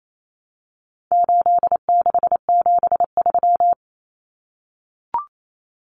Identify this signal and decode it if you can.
Morse code, heard as a keyed tone, about 28 wpm, 700 Hz. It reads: 8673
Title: Common Cut Numbers - 4-Digit 28wpm